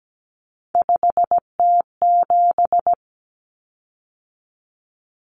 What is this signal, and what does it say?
Morse code, heard as a keyed tone, about 17 wpm, 700 Hz. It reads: 5T7
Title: Common Cut Numbers - 3-Digit 17wpm